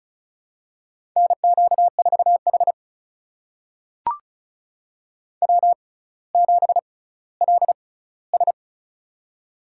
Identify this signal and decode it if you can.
Morse code, heard as a keyed tone, about 36 wpm, spaced as for 18 wpm, 700 Hz. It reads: NQ4H W7LS
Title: USA Call Signs - 4 Characters - 6x Character Spacing - 35wpm